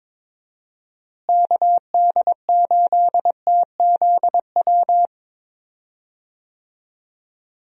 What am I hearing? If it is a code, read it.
Morse code, heard as a keyed tone, about 22 wpm, 700 Hz. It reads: KD8TZW
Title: USA Call Signs - 6 Characters - Repeated 3x - 22wpm